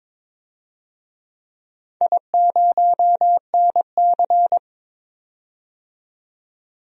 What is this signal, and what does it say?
Morse code, heard as a keyed tone, about 22 wpm, 700 Hz. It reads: I0NC
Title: DXCC-100 Call Signs - 4 Characters - 22wpm